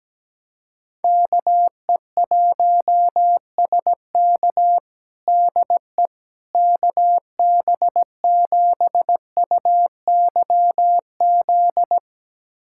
Morse code, heard as a keyed tone, about 17 wpm, 700 Hz. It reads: KE1SK DE KB7UYZ
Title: USA Callsign DE Callsign 17wpm